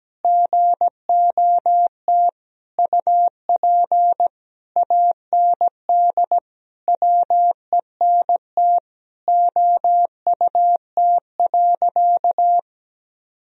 Morse code, heard as a keyed tone, about 17 wpm, 700 Hz. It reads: GOT UP AND WENT OUT.
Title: Sentences from Top 1000 Words - Review Part 1 - 17wpm